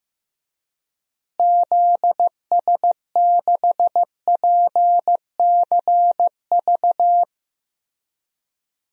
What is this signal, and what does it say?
Morse code, heard as a keyed tone, about 15 wpm, 700 Hz. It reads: ZS6PCV
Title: DXCC-100 Call Signs - 6 Characters - Repeated 2x - 15wpm